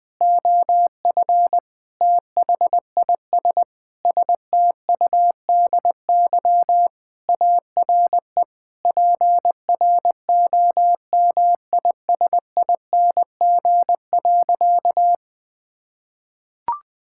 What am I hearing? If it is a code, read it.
Morse code, heard as a keyed tone, about 21 wpm, 700 Hz. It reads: OF THIS STUDY ARE PROMISING.
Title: Sentences with 5-Word Phrases - Spoken Beginning - 20wpm